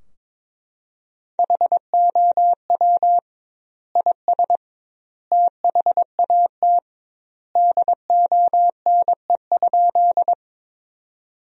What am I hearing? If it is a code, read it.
Morse code, heard as a keyed tone, about 22 wpm, 700 Hz. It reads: HOW IS THAT DONE?
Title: 2x Word-Spacing - Sentences from Top 500 Words - 22wpm